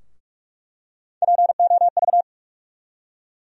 Morse code, heard as a keyed tone, about 46 wpm, 700 Hz. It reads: POV
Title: USA Call Sign Postfixes - 3 Letters 45wpm